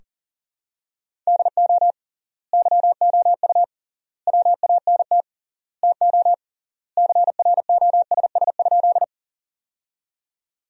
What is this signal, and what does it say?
Morse code, heard as a keyed tone, about 40 wpm, 700 Hz. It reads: DO YOU WANT TO CROSS?
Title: 3x Word-Spacing - Sentences from Top 300 Words - 40wpm